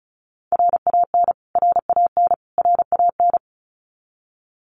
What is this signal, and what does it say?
Morse code, heard as a keyed tone, about 34 wpm, 700 Hz. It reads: RAN RAN RAN
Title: Top 500 Words Repeated 35wpm